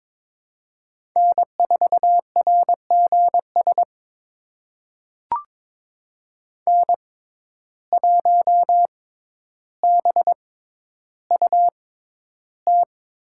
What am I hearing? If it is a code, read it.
Morse code, heard as a keyed tone, about 22 wpm, spaced as for 11 wpm, 700 Hz. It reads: N4RGS N1BUT
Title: USA Call Signs - 5 Characters - 6x Character Spacing - 22wpm